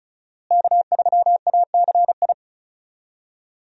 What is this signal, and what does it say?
Morse code, heard as a keyed tone, about 36 wpm, 700 Hz. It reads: K3ACI
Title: USA Call Signs - 5 Characters - Repeated 2x - 35wpm